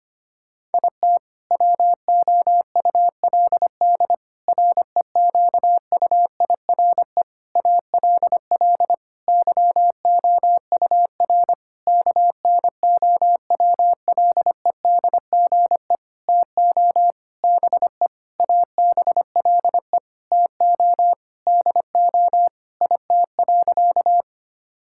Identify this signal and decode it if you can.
Morse code, heard as a keyed tone, about 25 wpm, 700 Hz. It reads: IT WOULD REQUIRE ALL YOUR KNOWLEDGE TO BE ABLE TO DO IT.
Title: Sentences from Top 1000 Words - Review Part 6 - 25wpm